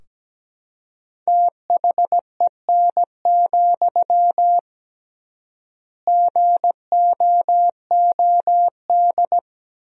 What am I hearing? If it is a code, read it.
Morse code, heard as a keyed tone, about 17 wpm, 700 Hz. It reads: THEN, GOOD
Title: 3x Word-Spacing - Sentences from Top 300 Words - 17wpm